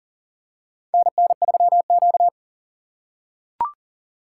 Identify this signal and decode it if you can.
Morse code, heard as a keyed tone, about 40 wpm, 700 Hz. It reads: NN3Q
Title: USA Call Signs - 4 Characters - 6x Character Spacing - 40wpm